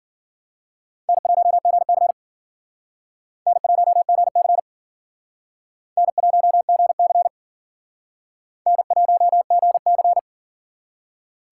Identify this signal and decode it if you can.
Morse code, heard as a keyed tone, about 54 wpm, 700 Hz. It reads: N1GC N1GC N1GC N1GC